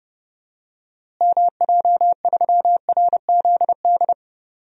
Morse code, heard as a keyed tone, about 30 wpm, 700 Hz. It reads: MJ3RZD
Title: DXCC-100 Call Signs - 6 Characters - Repeated 3x - 30wpm